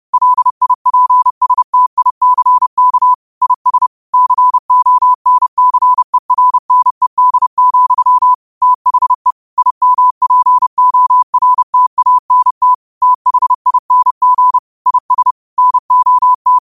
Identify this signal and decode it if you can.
Morse code, heard as a keyed tone, about 30 wpm, 1000 Hz. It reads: LIPSTICK IS CONCERNED, THE IMPORTANT THING IS NOT
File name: qotdfiles / 30wpm